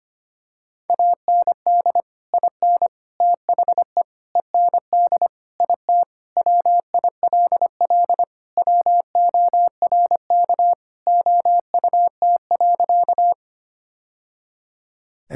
Morse code, heard as a keyed tone, about 25 wpm, 700 Hz. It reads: AND IN THE END IT WILL WORK OUT.
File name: practice / 25wpm / Sentences with 3-Word Phrases - 25wpm